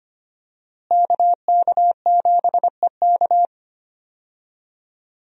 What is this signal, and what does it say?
Morse code, heard as a keyed tone, about 25 wpm, 700 Hz. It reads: KK7EK
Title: USA Call Signs - 5 Characters - Repeated 2x - 25wpm